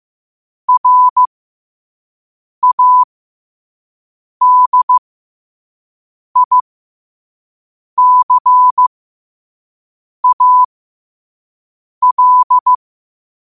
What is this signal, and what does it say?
Morse code, heard as a keyed tone, about 15 wpm, spaced as for 5 wpm, 1000 Hz. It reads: RADICAL
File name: qotdfiles / 5wpm